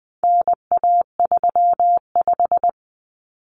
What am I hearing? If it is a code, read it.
Morse code, heard as a keyed tone, about 20 wpm, 700 Hz. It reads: NA35
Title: Common Cut Numbers - 4-Digit 20wpm